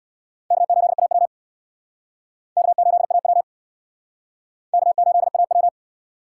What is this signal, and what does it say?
Morse code, heard as a keyed tone, about 58 wpm, 700 Hz. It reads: K8AW K8AW K8AW
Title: Speed Racing - USA 1x2 Call Signs 50wpm